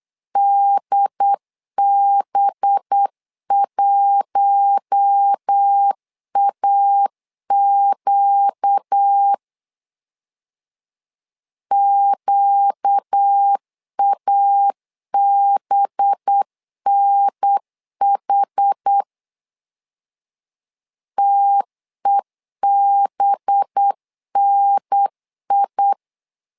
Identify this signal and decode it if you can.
Morse code, heard as a keyed tone, about 8 wpm, 800 Hz. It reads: DB1AQ QABNH TEBNI
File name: LEZIONI / 5 LBVC / 5 LB 8wpm 2816